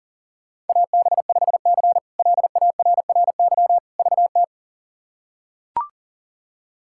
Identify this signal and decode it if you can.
Morse code, heard as a keyed tone, about 40 wpm, 690 Hz. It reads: AB5C LARRY VT
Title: K1USN SST QSO Practice - 40wpm